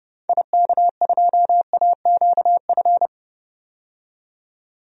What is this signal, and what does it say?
Morse code, heard as a keyed tone, about 32 wpm, 700 Hz. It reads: IK2AQF
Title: DXCC-100 Call Signs - 6 Characters - 30wpm